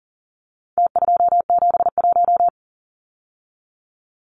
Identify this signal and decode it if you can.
Morse code, heard as a keyed tone, about 40 wpm, 700 Hz. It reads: T271
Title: Numbers - Contest Style - 4-Digit 40wpm